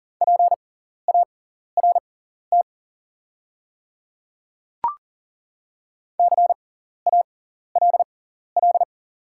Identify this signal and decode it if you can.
Morse code, heard as a keyed tone, about 40 wpm, spaced as for 14 wpm, 700 Hz. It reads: PART CALL
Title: 6x Character Spacing - Top 200 Words - 40wpm